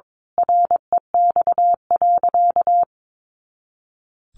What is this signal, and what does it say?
Morse code, heard as a keyed tone, about 22 wpm, 700 Hz. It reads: REX.
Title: A Sound Of Thunder 22wpm